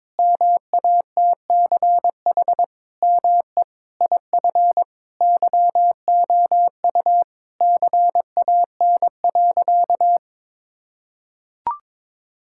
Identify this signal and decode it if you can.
Morse code, heard as a keyed tone, about 22 wpm, 700 Hz. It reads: MATCH ME IF YOU CAN.
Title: Sentences from Top 1600 Words 22wpm